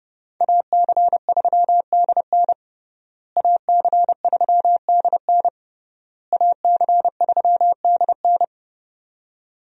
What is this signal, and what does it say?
Morse code, heard as a keyed tone, about 30 wpm, 700 Hz. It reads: AC3DN AC3DN AC3DN
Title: USA Call Signs - 5 Characters - Repeated 3x - 30wpm